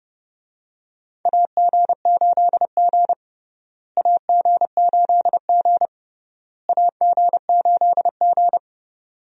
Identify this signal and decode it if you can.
Morse code, heard as a keyed tone, about 30 wpm, 700 Hz. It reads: AG8G AG8G AG8G
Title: USA Call Signs - 4 Characters - Repeated 3x - 30wpm